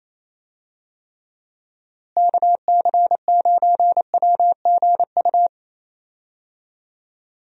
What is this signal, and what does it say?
Morse code, heard as a keyed tone, about 28 wpm, 700 Hz. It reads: KC9WGU
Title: USA Call Signs - 6 Characters - Repeated 3x - 28wpm